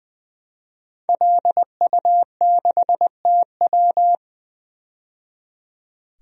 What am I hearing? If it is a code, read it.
Morse code, heard as a keyed tone, about 20 wpm, 700 Hz. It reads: LU6TW
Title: DXCC-100 Call Signs - 5 Characters - 20wpm